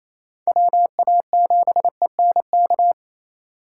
Morse code, heard as a keyed tone, about 35 wpm, 700 Hz. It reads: WA7ENK